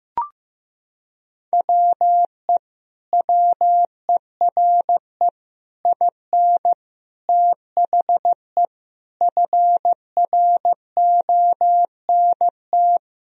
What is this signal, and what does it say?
Morse code, heard as a keyed tone, about 15 wpm, 700 Hz. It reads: WE WERE IN THE FRONT
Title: Sentences from Top 1000 Words - Review Part 2 - 15wpm